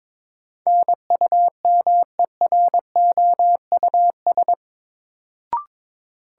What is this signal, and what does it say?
Morse code, heard as a keyed tone, about 22 wpm, 700 Hz. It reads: NUMEROUS
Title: Sentences from Top 1400 Words - New Word Drill - 22wpm